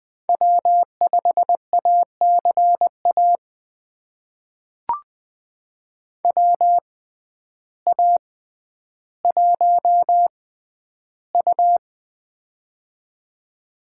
Morse code, heard as a keyed tone, about 20 wpm, spaced as for 11 wpm, 700 Hz. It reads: W5ACA WA1U
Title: USA Call Signs - 6x Character Spacing - 20wpm